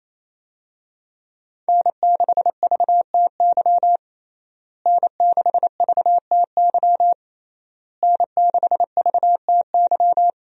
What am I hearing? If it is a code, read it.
Morse code, heard as a keyed tone, about 27 wpm, 700 Hz. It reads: N6VTY N6VTY N6VTY
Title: USA Call Signs - 5 Characters - Repeated 3x - 28wpm